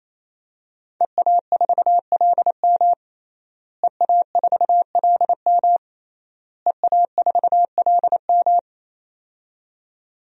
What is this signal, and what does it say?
Morse code, heard as a keyed tone, about 28 wpm, 700 Hz. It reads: EA4LM EA4LM EA4LM
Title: DXCC-100 Call Signs - 5 Characters - Repeated 3x - 28wpm